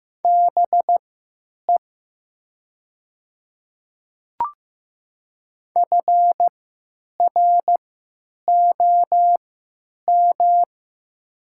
Morse code, heard as a keyed tone, about 15 wpm, spaced as for 9 wpm, 700 Hz. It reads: BE FROM
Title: 3x Character Spacing - Top 300 Words - 15wpm